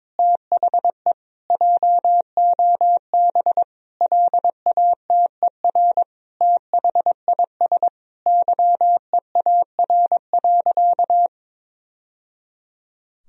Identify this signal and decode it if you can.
Morse code, heard as a keyed tone, about 22 wpm, 700 Hz. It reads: THE JOB LATER THIS YEAR.
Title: Sentences from Top 1000 Words - Review Part 3 - 22wpm